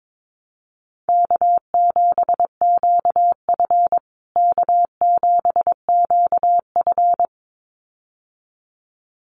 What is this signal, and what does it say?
Morse code, heard as a keyed tone, about 22 wpm, 700 Hz. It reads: K7QF K7QF
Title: USA Call Signs Repeated 2x 22wpm